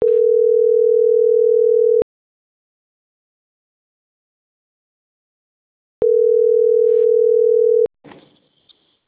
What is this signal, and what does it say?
Ringback tone (North America), 440 + 480 Hz.